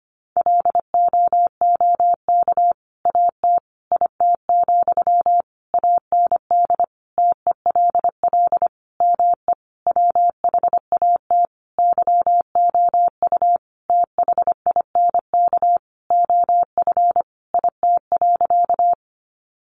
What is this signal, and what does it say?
Morse code, heard as a keyed tone, about 25 wpm, 700 Hz. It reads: LOOK AT IT, AND TELL ME WHAT YOU THINK OF IT.
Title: Sentences from Top 200 Words 25wpm